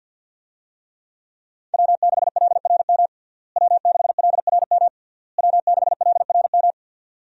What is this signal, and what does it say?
Morse code, heard as a keyed tone, about 51 wpm, 700 Hz. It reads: W6LRM W6LRM W6LRM
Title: USA Call Signs - 5 Characters - Repeated 3x - 50wpm